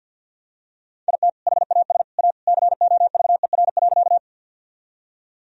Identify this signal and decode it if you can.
Morse code, heard as a keyed tone, about 51 wpm, 700 Hz. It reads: IT HAS A COVER.
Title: Sentences from Top 1000 Words - Review Part 1 - 50wpm